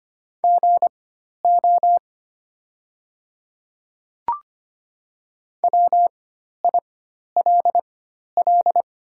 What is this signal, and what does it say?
Morse code, heard as a keyed tone, about 25 wpm, spaced as for 12 wpm, 690 Hz. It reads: GO WILL
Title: 4x Character Spacing - Top 100 Words - 25wpm